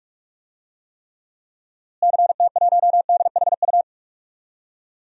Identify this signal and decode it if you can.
Morse code, heard as a keyed tone, about 46 wpm, 700 Hz. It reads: CT1DHU
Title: DXCC-100 Call Signs - 6 Characters - Repeated 3x - 45wpm